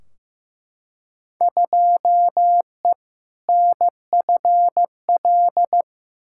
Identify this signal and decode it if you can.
Morse code, heard as a keyed tone, about 15 wpm, 700 Hz. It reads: 2E NFL
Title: Field Day Exchanges Encore - 15wpm